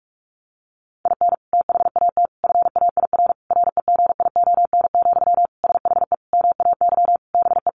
Morse code, heard as a keyed tone, about 44 wpm, 700 Hz. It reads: IN THAT FAIR REGION, SHE MAY BE
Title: Sentences from Top 1500 Words 45wpm